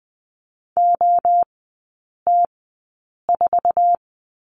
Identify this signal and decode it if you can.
Morse code, heard as a keyed tone, about 20 wpm, spaced as for 8 wpm, 690 Hz. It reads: OT4
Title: 081 Lesson 14e - Number 2 - Three Characters 20wpm